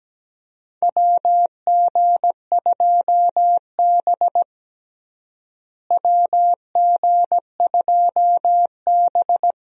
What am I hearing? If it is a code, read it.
Morse code, heard as a keyed tone, about 17 wpm, 700 Hz. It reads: WG2B WG2B
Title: USA Call Signs - 4 Characters - Repeated 2x - 17wpm